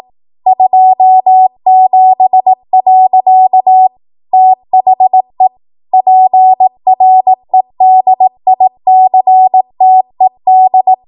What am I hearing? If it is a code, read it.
Morse code, heard as a keyed tone, about 18 wpm, 750 Hz. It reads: 27. THE PREDICTED